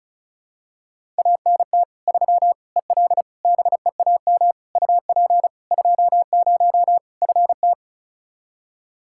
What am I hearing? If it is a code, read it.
Morse code, heard as a keyed tone, about 36 wpm, 700 Hz. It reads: ANT 3 EL BEAM UP 20 FT
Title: Practice QSO - Exchange 2 - Info - 35wpm